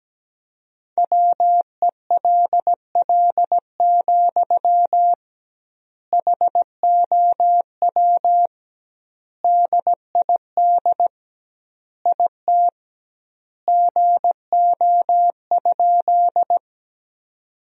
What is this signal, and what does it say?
Morse code, heard as a keyed tone, about 17 wpm, 700 Hz. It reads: WELL, HOW DID IT GO?
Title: 2x Word-Spacing - Sentences from Top 200 Words - 17wpm